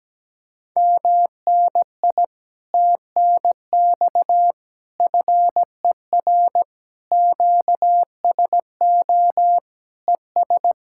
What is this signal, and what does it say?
Morse code, heard as a keyed tone, about 17 wpm, 700 Hz. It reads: MNI TNX FER QSO ES
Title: Speed-Racing Common QSO Phrases - 17wpm